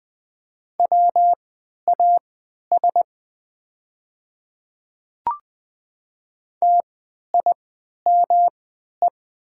Morse code, heard as a keyed tone, about 20 wpm, spaced as for 11 wpm, 700 Hz. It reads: WAS TIME
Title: 3x Character Spacing - Top 100 Words - 20wpm